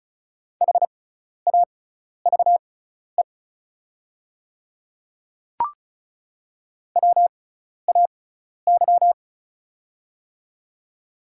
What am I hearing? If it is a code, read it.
Morse code, heard as a keyed tone, about 34 wpm, spaced as for 12 wpm, 700 Hz. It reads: HAVE WAY
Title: 6x Character Spacing - Top 100 Words - 35wpm